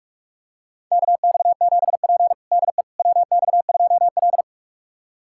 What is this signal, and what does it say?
Morse code, heard as a keyed tone, about 46 wpm, 700 Hz. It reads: KX7P DE WX2L
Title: USA Callsign DE Callsign - 4 Characters - 45wpm